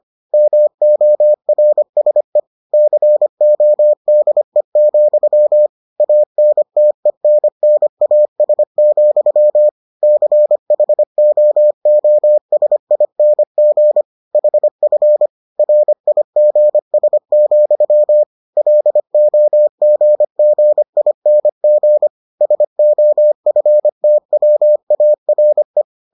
Morse code, heard as a keyed tone, about 25 wpm, 600 Hz. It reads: MORSE CODE, ANTENNAS, CHOOSING HF RIGS, LOGGING SOFTWARE